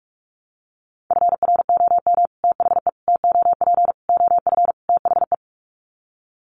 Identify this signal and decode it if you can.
Morse code, heard as a keyed tone, about 44 wpm, 700 Hz. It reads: FROM THE TOP OF THE
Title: Sets of 5 Words Encore 45wpm